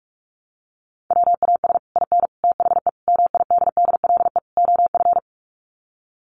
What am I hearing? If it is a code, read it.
Morse code, heard as a keyed tone, about 44 wpm, 700 Hz. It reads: WAS IN THE MIDDLE OF
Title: Sets of 5 Words Encore 45wpm